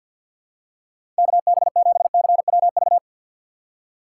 Morse code, heard as a keyed tone, about 47 wpm, 700 Hz. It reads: KB7CWV
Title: USA Call Signs - 3x Character Spacing - 50wpm